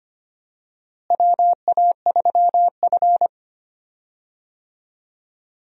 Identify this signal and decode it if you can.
Morse code, heard as a keyed tone, about 25 wpm, 700 Hz. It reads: WA3F
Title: USA Call Signs - 4 Characters - Repeated 3x - 25wpm